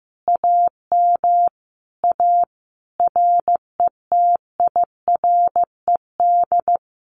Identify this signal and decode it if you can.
Morse code, heard as a keyed tone, about 15 wpm, 700 Hz. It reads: AM A RETIRED
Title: Common QSO Phrases Encore 15wpm